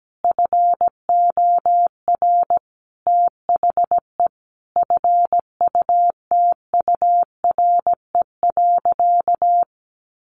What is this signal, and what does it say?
Morse code, heard as a keyed tone, about 17 wpm, 700 Hz. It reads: FOR THE FUTURE.
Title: Sentences from Top 1400 Words 17wpm